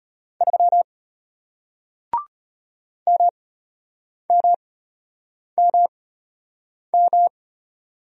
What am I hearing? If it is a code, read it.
Morse code, heard as a keyed tone, about 33 wpm, spaced as for 8 wpm, 700 Hz. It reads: 3 MMMM